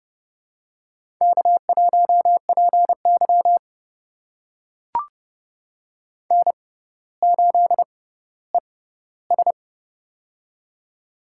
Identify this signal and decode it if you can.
Morse code, heard as a keyed tone, about 30 wpm, spaced as for 15 wpm, 700 Hz. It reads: K1PY N8ES
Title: USA Call Signs - 4 Characters - 6x Character Spacing - 30wpm